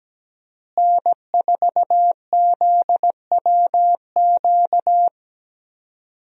Morse code, heard as a keyed tone, about 17 wpm, 700 Hz. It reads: N4ZWQ
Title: USA Call Signs - 5 Characters - Repeated 3x - 17wpm